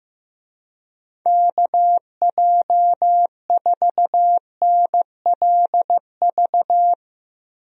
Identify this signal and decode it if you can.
Morse code, heard as a keyed tone, about 16 wpm, 700 Hz. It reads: KJ4NLV
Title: USA Call Signs - 6 Characters - Repeated 2x - 15wpm